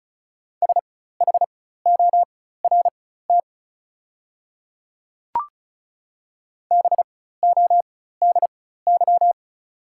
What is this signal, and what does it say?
Morse code, heard as a keyed tone, about 36 wpm, spaced as for 16 wpm, 690 Hz. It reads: SHORT BODY
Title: 4x Character Spacing - Top 500 Words - 35wpm